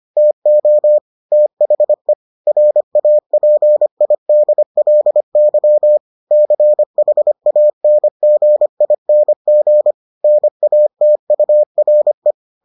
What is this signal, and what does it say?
Morse code, heard as a keyed tone, about 25 wpm, 600 Hz. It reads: TO THE RAPIDLY CHANGING NATURE